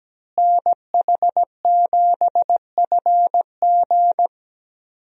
Morse code, heard as a keyed tone, about 17 wpm, 700 Hz. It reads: NH7FG
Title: USA Call Signs - 5 Characters - Repeated 2x - 17wpm